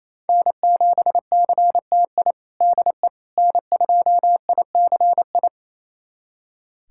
Morse code, heard as a keyed tone, about 28 wpm, 700 Hz. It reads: N7CTI DE N2ICI
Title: USA Callsign DE Callsign - 5 Characters - 28wpm